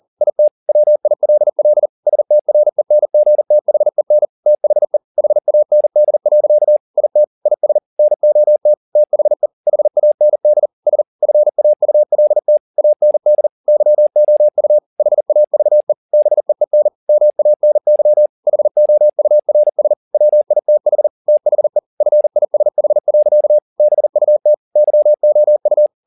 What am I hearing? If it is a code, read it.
Morse code, heard as a keyed tone, about 40 wpm, 600 Hz. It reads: IT WILL STRENGTHEN THE HAND. IT IS NOT THE HAND S FAULT AND YOU HAVE BEEN MANY HOURS WITH THE FISH. BUT YOU